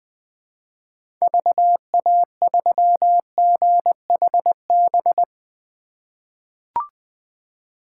Morse code, heard as a keyed tone, about 20 wpm, 700 Hz. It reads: VA3GHB
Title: 2023 POTA Activator Call Signs - 20wpm